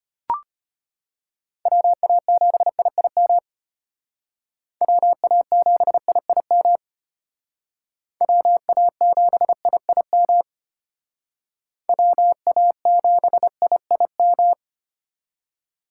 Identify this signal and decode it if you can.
Morse code, heard as a keyed tone, about 33 wpm, 700 Hz. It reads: WA7IIM WA7IIM WA7IIM WA7IIM